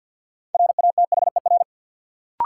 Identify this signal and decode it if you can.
Morse code, heard as a keyed tone, about 54 wpm, 700 Hz. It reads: RATHER
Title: Sentences from Top 700 Words - New Word Drill - 50wpm